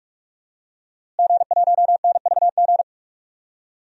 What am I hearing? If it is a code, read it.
Morse code, heard as a keyed tone, about 46 wpm, 700 Hz. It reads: G1NVG
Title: DXCC-100 Call Signs - 5 Characters - Repeated 2x - 45wpm